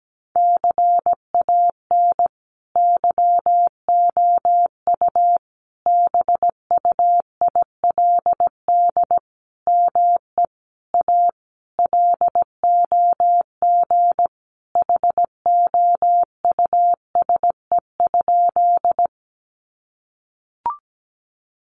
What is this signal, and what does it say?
Morse code, heard as a keyed tone, about 17 wpm, 700 Hz. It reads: CAN YOU BUILD ME A LOG HOUSE?
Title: Sentences from Top 1400 Words 17wpm